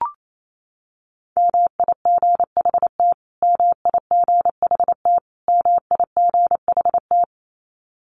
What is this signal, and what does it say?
Morse code, heard as a keyed tone, about 28 wpm, 700 Hz. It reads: MIGHT MIGHT MIGHT
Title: Top 500 Words Repeated 28wpm